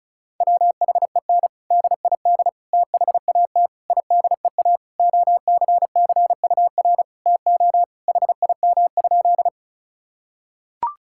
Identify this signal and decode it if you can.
Morse code, heard as a keyed tone, about 36 wpm, 700 Hz. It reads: WHEN DID THAT IDEA OCCUR TO HIM?
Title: Sentences from Top 1000 Words - Review Part 6 - 35wpm